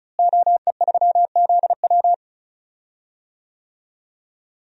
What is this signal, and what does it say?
Morse code, heard as a keyed tone, about 34 wpm, 700 Hz. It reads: OE3ZW
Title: DXCC-100 Call Signs - 5 Characters - Repeated 3x - 35wpm